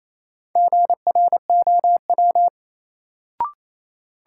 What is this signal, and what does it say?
Morse code, heard as a keyed tone, about 28 wpm, 700 Hz. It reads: GROW
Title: Sentences from Top 300 Words - New Word Drill - 28wpm